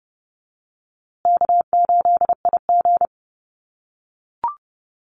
Morse code, heard as a keyed tone, about 29 wpm, 700 Hz. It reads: K8IG
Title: USA Call Signs Repeated 3x 30wpm